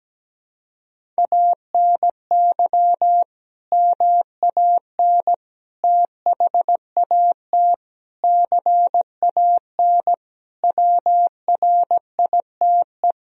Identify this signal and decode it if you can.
Morse code, heard as a keyed tone, about 17 wpm, 700 Hz. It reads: ANY MAN THAT CAN WRITE
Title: Sentences from Top 1000 Words - Review Part 1 - 17wpm